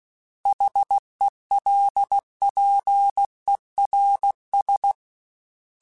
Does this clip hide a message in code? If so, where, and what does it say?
Yes: Morse code from 0.5 to 4.9 s, heard as a keyed tone, about 16 wpm, 800 Hz. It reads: HELPERS